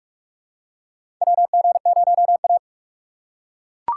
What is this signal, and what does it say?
Morse code, heard as a keyed tone, about 42 wpm, 700 Hz. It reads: WG0A
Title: USA Call Signs - 4 Characters - 2x Character Spacing - 45wpm